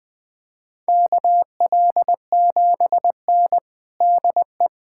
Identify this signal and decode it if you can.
Morse code, heard as a keyed tone, about 20 wpm, 700 Hz. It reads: KL7N DE
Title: USA Callsign DE Callsign - 4 Characters - 20wpm